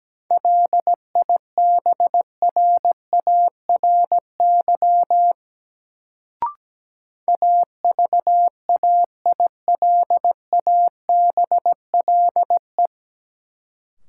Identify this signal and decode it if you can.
Morse code, heard as a keyed tone, about 17 wpm, 700 Hz. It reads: LIBRARY AVAILABLE
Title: Sentences from Top 600 Words - New Word Drill - 17wpm